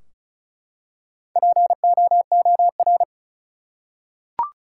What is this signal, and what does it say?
Morse code, heard as a keyed tone, about 36 wpm, 700 Hz. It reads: POOR
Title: Four Letter Words - 35wpm